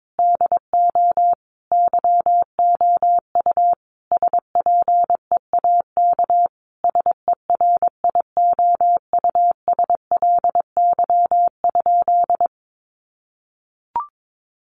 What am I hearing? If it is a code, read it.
Morse code, heard as a keyed tone, about 22 wpm, 700 Hz. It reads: DO YOU SPEAK SERIOUSLY?
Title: Sentences from Top 1200 Words 22wpm